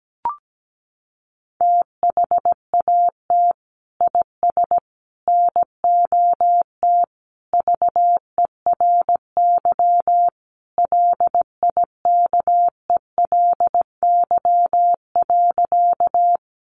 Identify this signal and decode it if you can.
Morse code, heard as a keyed tone, about 17 wpm, 700 Hz. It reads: THAT IS NOT VERY LIKELY.
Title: Sentences from Top 600 Words 17wpm